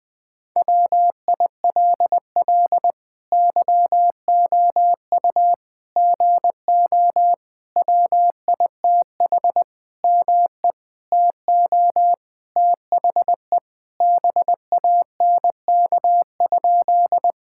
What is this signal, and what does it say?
Morse code, heard as a keyed tone, about 20 wpm, 700 Hz. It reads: WILL YOU GO WITH ME TO THE BANK?
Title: Sentences from Top 1000 Words - Review Part 6 - 20wpm